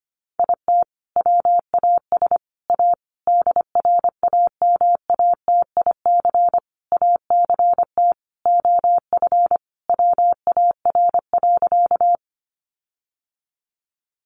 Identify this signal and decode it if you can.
Morse code, heard as a keyed tone, about 25 wpm, 700 Hz. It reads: IT WAS A DRAMATIC ACT OF WAR.
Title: Sentences from Top 1300 Words 25wpm